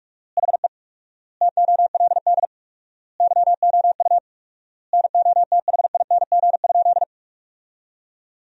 Morse code, heard as a keyed tone, about 47 wpm, 690 Hz. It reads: HE TOLD YOU NOTHING?
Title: 4x Word-Spacing - Sentences from Top 500 Words - 45wpm